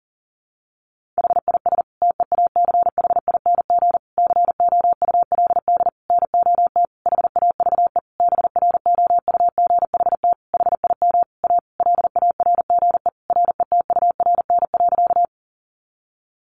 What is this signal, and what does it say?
Morse code, heard as a keyed tone, about 40 wpm, 700 Hz. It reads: HIS TEACHING COULD NOT HAVE BROUGHT HIM A LARGE RETURN.
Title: Sentences from Top 1100 Words 40wpm